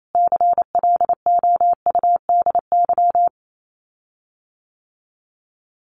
Morse code, heard as a keyed tone, about 28 wpm, 700 Hz. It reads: CLOUDY
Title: Common QSO Phrases Encore 28wpm